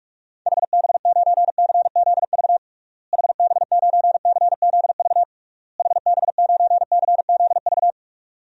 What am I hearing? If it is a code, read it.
Morse code, heard as a keyed tone, about 41 wpm, 700 Hz. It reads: HB9CZV HB9CZV HB9CZV
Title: DXCC-100 Call Signs - 6 Characters - Repeated 3x - 45wpm